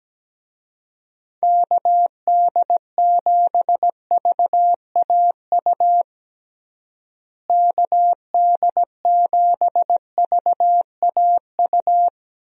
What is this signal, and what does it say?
Morse code, heard as a keyed tone, about 17 wpm, 700 Hz. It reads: KD7VAU KD7VAU
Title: USA Call Signs - 6 Characters - Repeated 2x - 17wpm